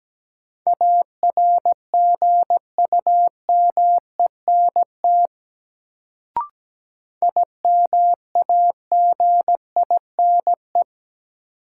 Morse code, heard as a keyed tone, about 17 wpm, 700 Hz. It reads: ARGUMENT IMAGINE
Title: Sentences from Top 1400 Words - New Word Drill Alt - 17wpm